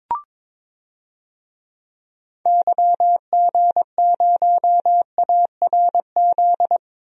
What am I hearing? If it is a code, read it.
Morse code, heard as a keyed tone, about 22 wpm, 700 Hz. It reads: YG0ARZ
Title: DXCC-100 Call Signs - 6 Characters - 22wpm